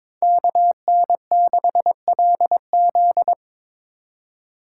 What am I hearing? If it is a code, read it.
Morse code, heard as a keyed tone, about 22 wpm, 700 Hz. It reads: KN6LZ
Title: USA Call Signs - 4x Character Spacing - 22wpm